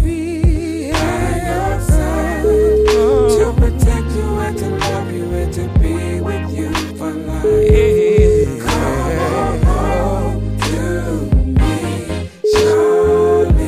Ringback tone (Europe, 425 Hz), with noise.